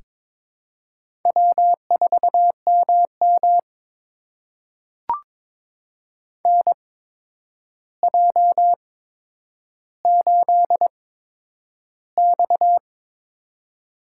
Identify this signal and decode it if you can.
Morse code, heard as a keyed tone, about 22 wpm, spaced as for 9 wpm, 700 Hz. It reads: W4MM NJ8X
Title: USA Call Signs - 4 Characters - 8x Character Spacing - 22wpm